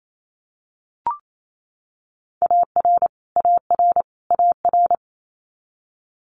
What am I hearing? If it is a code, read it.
Morse code, heard as a keyed tone, about 28 wpm, 700 Hz. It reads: AR AR AR
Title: Multi-letter Phonemes Repeated 3x 28wpm